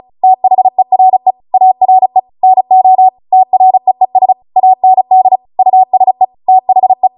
Morse code, heard as a keyed tone, about 36 wpm, 750 Hz. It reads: THERE ARE NO TREES AND USE THE
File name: Morse / 35WPM